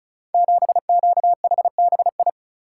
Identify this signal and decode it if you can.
Morse code, heard as a keyed tone, about 36 wpm, 700 Hz. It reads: 7QHBI
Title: Code Groups - 5 Letter:Numbers - 35wpm